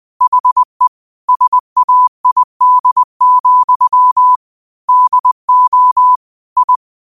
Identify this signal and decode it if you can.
Morse code, heard as a keyed tone, about 20 wpm, 1000 Hz. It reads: HE SAID, DO I
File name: 2025-06-25-20wpm